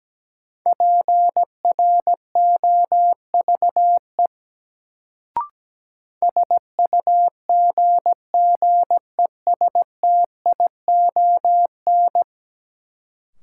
Morse code, heard as a keyed tone, about 17 wpm, 700 Hz. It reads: PROVE SUGGESTION
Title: Sentences from Top 1600 Words - New Word Drill Alt - 17wpm